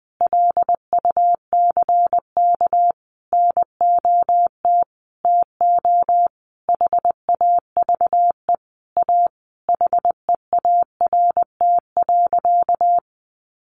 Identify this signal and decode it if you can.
Morse code, heard as a keyed tone, about 20 wpm, 700 Hz. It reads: LUCK NOT TO HAVE A HEART.
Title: Sentences from Top 1500 Words 20wpm